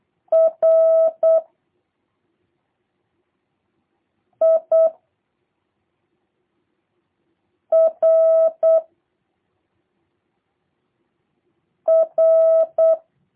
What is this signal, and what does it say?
Morse code, heard as a keyed tone, about 8 wpm, 650 Hz. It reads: R I R R
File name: jh8cbh / 2023cw / review1-2